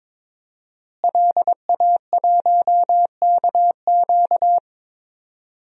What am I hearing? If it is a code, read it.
Morse code, heard as a keyed tone, about 22 wpm, 700 Hz. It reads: LA1KQ
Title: DXCC-100 Call Signs - 5 Characters - Repeated 3x - 22wpm